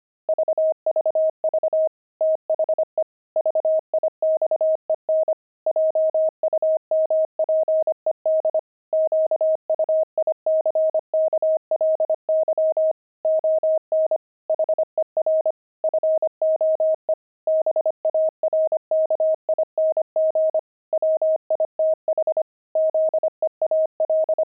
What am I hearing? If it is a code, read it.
Morse code, heard as a keyed tone, about 25 wpm, 630 Hz. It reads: VVV THE VIXEN JUMPED QUICKLY ON HER FOE BARKING WITH ZEAL
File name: pangram-fs-25-cs-25-barking